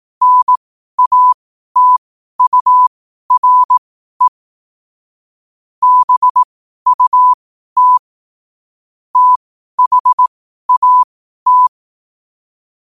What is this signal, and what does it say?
Morse code, heard as a keyed tone, about 18 wpm, spaced as for 12 wpm, 1000 Hz. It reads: NATURE BUT THAT
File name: qotdfiles / 12wpm